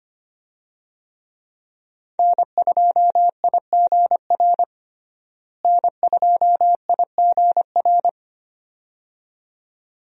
Morse code, heard as a keyed tone, about 25 wpm, 700 Hz. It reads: N2IGR N2IGR
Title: USA Call Signs - 5 Characters - Repeated 2x - 25wpm